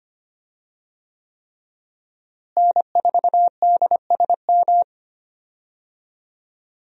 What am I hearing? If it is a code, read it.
Morse code, heard as a keyed tone, about 25 wpm, 700 Hz. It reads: N4DSM
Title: USA Call Signs - 5 Characters - Repeated 3x - 25wpm